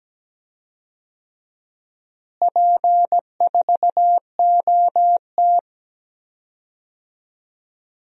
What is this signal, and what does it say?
Morse code, heard as a keyed tone, about 17 wpm, 700 Hz. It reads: P4OT
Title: DXCC-100 Call Signs - 4 Characters - Repeated 2x - 17wpm